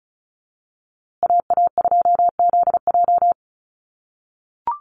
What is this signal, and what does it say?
Morse code, heard as a keyed tone, about 34 wpm, 700 Hz. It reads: AA2ZJ
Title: USA Call Signs Repeated 2x 35wpm